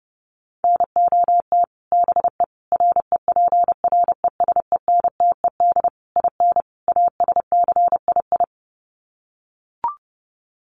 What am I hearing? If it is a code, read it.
Morse code, heard as a keyed tone, about 30 wpm, 700 Hz. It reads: NOT BE REPRESENTED IN ASCII
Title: Sets of 5 Words 30wpm